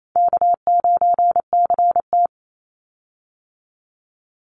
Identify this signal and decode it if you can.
Morse code, heard as a keyed tone, about 28 wpm, 700 Hz. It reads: K9CT
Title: Call Signs - 28wpm